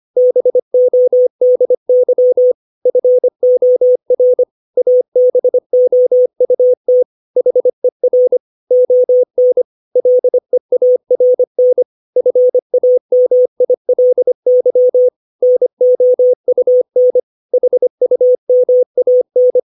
Morse code, heard as a keyed tone, about 25 wpm, 500 Hz. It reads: BODY FOR ABOUT HER ON LEARN FAMILY NOUN HUMAN
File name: 632 words / 632 Words 25WPM